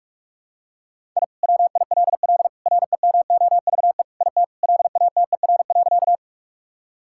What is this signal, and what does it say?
Morse code, heard as a keyed tone, about 45 wpm, 700 Hz. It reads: I WILL REMOVE IT LATER.
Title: Sentences from Top 1000 Words - Review Part 5 - 45wpm